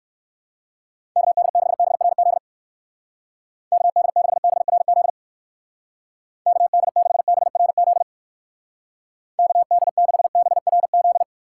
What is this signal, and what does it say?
Morse code, heard as a keyed tone, about 60 wpm, 700 Hz. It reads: KD6BRZ KD6BRZ KD6BRZ KD6BRZ